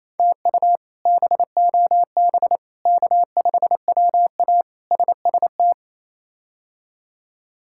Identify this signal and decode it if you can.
Morse code, heard as a keyed tone, about 28 wpm, 700 Hz. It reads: TU BOB K5WA SST
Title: K1USN SST QSO Practice - 28wpm